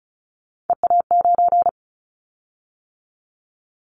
Morse code, heard as a keyed tone, about 34 wpm, 700 Hz. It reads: EA9
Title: Callsign Prefixes for Top 100 DXCC - 35wpm